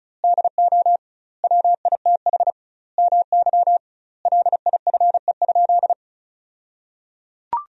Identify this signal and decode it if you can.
Morse code, heard as a keyed tone, about 36 wpm, 700 Hz. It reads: DO WITH MY LIFE?
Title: 2x Word-Spacing - Sentences from Top 300 Words - 35wpm